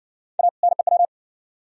Morse code, heard as a keyed tone, about 58 wpm, 700 Hz. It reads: A NEW